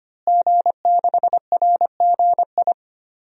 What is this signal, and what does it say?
Morse code, heard as a keyed tone, about 25 wpm, 700 Hz. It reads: G6RGI
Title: DXCC-100 Call Signs - 5 Characters - Repeated 2x - 25wpm